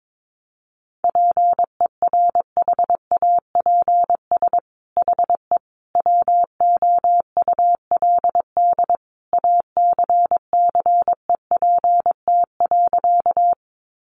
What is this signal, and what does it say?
Morse code, heard as a keyed tone, about 22 wpm, 700 Hz. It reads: PERHAPS HE WOULD ACCEPT.
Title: Sentences from Top 1300 Words 22wpm